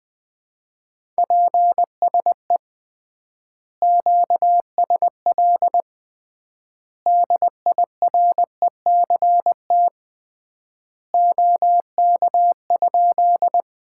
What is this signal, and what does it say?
Morse code, heard as a keyed tone, about 20 wpm, 700 Hz. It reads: PSE QSL DIRECT OK?
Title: Common QSO Phrases - 3x Word-Spacing - 20wpm